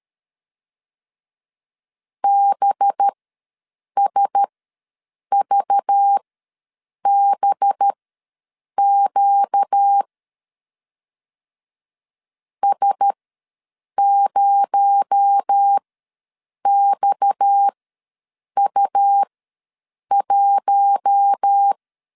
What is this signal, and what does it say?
Morse code, heard as a keyed tone, about 13 wpm, spaced as for 7 wpm, 800 Hz. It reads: BSVBQ S0XU1
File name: LEZIONI / 7 GF9Z / 7 GF ws 12wpm 3140